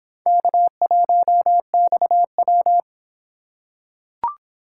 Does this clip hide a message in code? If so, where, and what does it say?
Yes: Morse code from 0.3 to 2.8 s, heard as a keyed tone, about 26 wpm, 700 Hz. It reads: K1XW